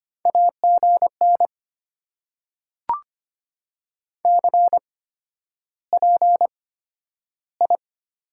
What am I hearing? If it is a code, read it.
Morse code, heard as a keyed tone, about 25 wpm, spaced as for 11 wpm, 700 Hz. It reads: AGN CPI
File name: Top 90 QSO Elements - 8x Character Spacing - 25wpm